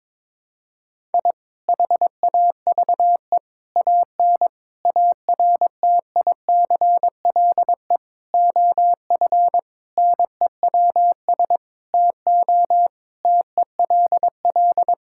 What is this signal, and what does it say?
Morse code, heard as a keyed tone, about 22 wpm, 700 Hz. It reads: I HAVE AN ARTICLE OF NEWS TO TELL
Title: Sentences from Top 1000 Words - Review Part 5 - 22wpm